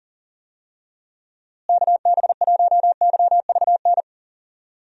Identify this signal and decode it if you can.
Morse code, heard as a keyed tone, about 40 wpm, 700 Hz. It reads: KB1YVN
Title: USA Call Signs - 6 Characters - Repeated 2x - 40wpm